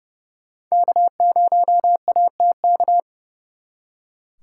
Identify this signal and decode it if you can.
Morse code, heard as a keyed tone, about 40 wpm, spaced as for 35 wpm, 700 Hz. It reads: K0ATK
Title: Speed Racing - USA 1x3 Call Signs 30wpm